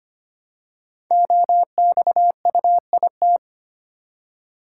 Morse code, heard as a keyed tone, about 25 wpm, 700 Hz. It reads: OXUIT
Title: Code Groups - 5 Letters - 25wpm